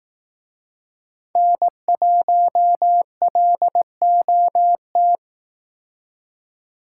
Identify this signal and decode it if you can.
Morse code, heard as a keyed tone, about 18 wpm, 700 Hz. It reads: N1LOT
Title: Speed Racing - USA 1x3 Call Signs 15wpm